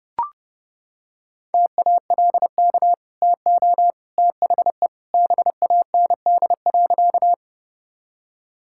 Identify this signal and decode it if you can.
Morse code, heard as a keyed tone, about 32 wpm, 700 Hz. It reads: TALK TO THE BAND.
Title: Sentences from Top 1600 Words 30wpm